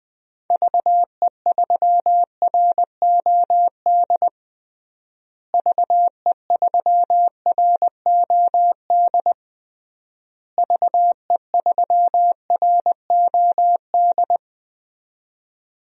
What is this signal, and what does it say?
Morse code, heard as a keyed tone, about 20 wpm, 700 Hz. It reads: VE3ROD VE3ROD VE3ROD
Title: DXCC-100 Call Signs - 6 Characters - Repeated 3x - 20wpm